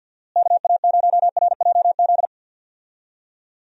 Morse code, heard as a keyed tone, about 48 wpm, 700 Hz. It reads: KA0RJZ
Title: USA Call Signs - 6 Characters - Repeated 3x - 50wpm